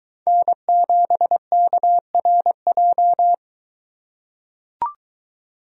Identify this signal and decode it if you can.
Morse code, heard as a keyed tone, about 21 wpm, 700 Hz. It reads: N7KRJ